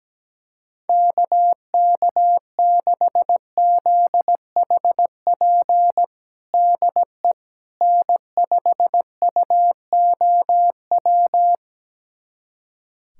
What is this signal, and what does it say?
Morse code, heard as a keyed tone, about 17 wpm, 700 Hz. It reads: KK6ZHP DE N5UOW
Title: USA Callsign DE Callsign 17wpm